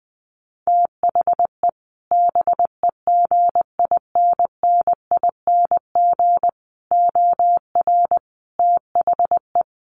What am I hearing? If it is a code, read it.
Morse code, heard as a keyed tone, about 20 wpm, 700 Hz. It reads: THE BEGINNING OR THE
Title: Sentences from Top 800 Words 20wpm